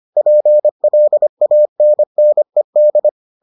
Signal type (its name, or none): Morse code